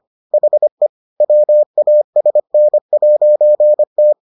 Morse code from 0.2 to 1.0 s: HE

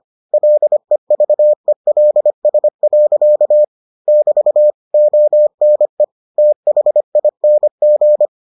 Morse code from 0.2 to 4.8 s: LEVELS. <BT>

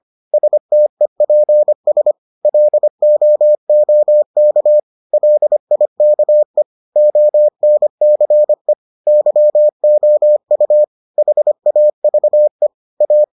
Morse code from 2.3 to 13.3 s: LOOK LIKE ONCE YOU HAVE A

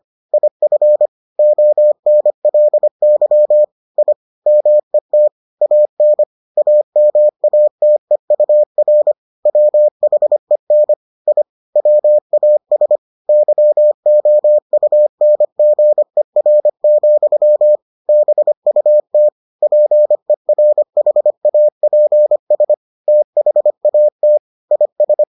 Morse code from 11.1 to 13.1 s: I WAS

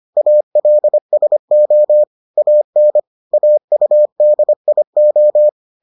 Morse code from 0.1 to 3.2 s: ALSO AN

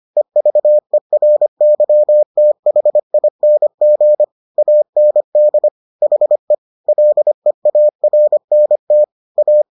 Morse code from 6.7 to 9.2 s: LEARNT

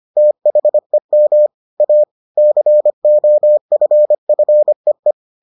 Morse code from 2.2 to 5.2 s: COFFEE

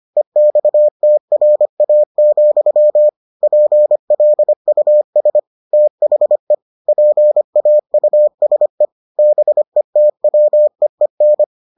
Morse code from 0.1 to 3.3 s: EXTRA,

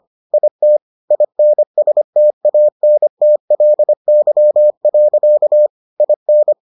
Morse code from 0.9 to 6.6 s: INSTANTLY. IN

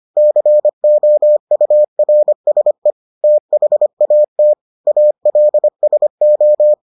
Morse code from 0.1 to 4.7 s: COURSE THAT

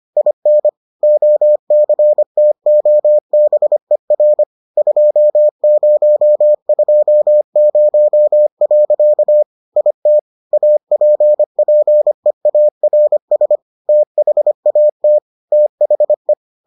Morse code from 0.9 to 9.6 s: OCTOBER 2020.

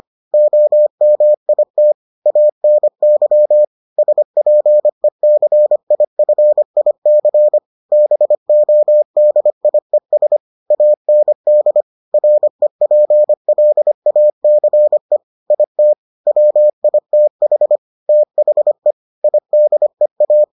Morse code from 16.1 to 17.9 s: WITH

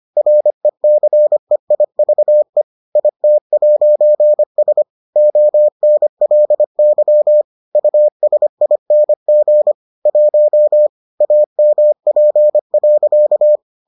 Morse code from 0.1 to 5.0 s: RECEIVE IT'S

